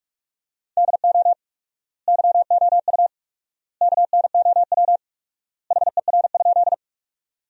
Morse code, heard as a keyed tone, about 46 wpm, 700 Hz. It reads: DO YOU KNOW HER?
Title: 4x Word-Spacing - Sentences from Top 100 Words - 45wpm